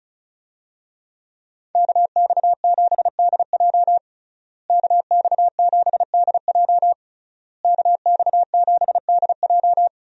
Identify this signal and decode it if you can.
Morse code, heard as a keyed tone, about 36 wpm, 700 Hz. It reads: KX7DJ KX7DJ KX7DJ
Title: USA Call Signs - 5 Characters - Repeated 3x - 35wpm